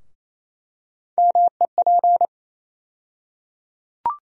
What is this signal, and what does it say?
Morse code, heard as a keyed tone, about 28 wpm, 700 Hz. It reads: MEP
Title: USA Call Sign Postfixes - 3 Letters 28wpm